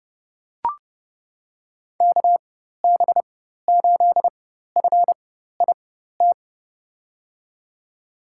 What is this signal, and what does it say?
Morse code, heard as a keyed tone, about 31 wpm, spaced as for 14 wpm, 700 Hz. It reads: KB8FIT
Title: USA Call Signs - 6 Characters - 4x Character Spacing - 30wpm